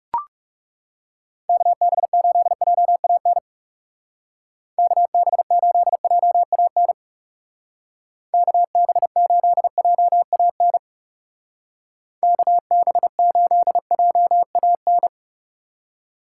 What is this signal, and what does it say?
Morse code, heard as a keyed tone, about 36 wpm, 700 Hz. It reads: KB8JAN KB8JAN KB8JAN KB8JAN
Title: Speed Racing - USA 2x3 Call Signs 30wpm